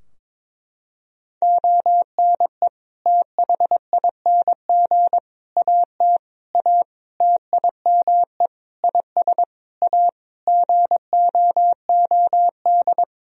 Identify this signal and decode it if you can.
Morse code, heard as a keyed tone, about 22 wpm, 700 Hz. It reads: ONE THING AT A TIME IS A GOOD
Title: Sentences from Top 1000 Words - Review Part 2 - 22wpm